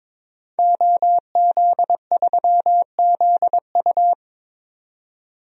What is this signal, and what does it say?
Morse code, heard as a keyed tone, about 22 wpm, 700 Hz. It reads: OZ3ZU
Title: DXCC-100 Call Signs - 5 Characters - 22wpm